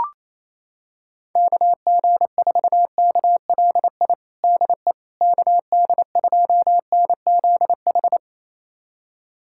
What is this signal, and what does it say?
Morse code, heard as a keyed tone, about 28 wpm, 700 Hz. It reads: KG4KLI DE KD2NZH
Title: USA Callsign DE Callsign 28wpm